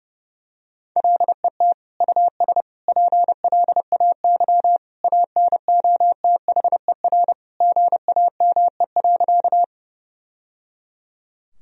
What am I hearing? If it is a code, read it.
Morse code, heard as a keyed tone, about 30 wpm, 700 Hz. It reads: LET US PLAY ANOTHER GAME.
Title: Sentences from Top 1000 Words - Review Part 3 - 30wpm